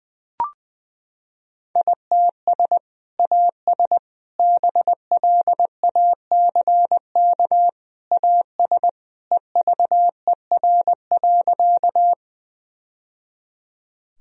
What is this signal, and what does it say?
Morse code, heard as a keyed tone, about 20 wpm, 700 Hz. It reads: ITS AS BLACK AS EVER.
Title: Sentences from Top 1000 Words - Review Part 2 - 20wpm